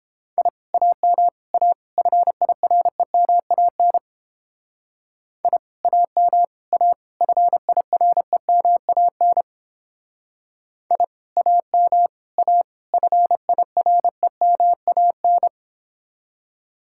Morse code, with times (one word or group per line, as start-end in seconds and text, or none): 0.4-0.5 I
0.7-1.3 AM
1.5-1.7 A
2.0-4.0 FIREMAN
5.4-5.6 I
5.8-6.4 AM
6.7-6.9 A
7.2-9.4 FIREMAN
10.9-11.0 I
11.4-12.1 AM
12.4-12.6 A
12.9-15.5 FIREMAN